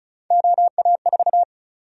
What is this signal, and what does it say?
Morse code, heard as a keyed tone, about 36 wpm, 700 Hz. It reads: OA4
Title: USA Call Sign Prefixes - Repeated 2x - 35wpm